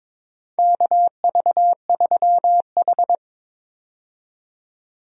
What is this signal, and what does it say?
Morse code, heard as a keyed tone, about 22 wpm, 700 Hz. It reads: KV3H
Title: USA Call Signs - 4 Characters - 22wpm